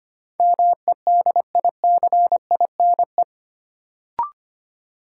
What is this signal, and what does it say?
Morse code, heard as a keyed tone, about 25 wpm, 700 Hz. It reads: MEDICINE
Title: Sentences from Top 900 Words - New Word Drill - 25wpm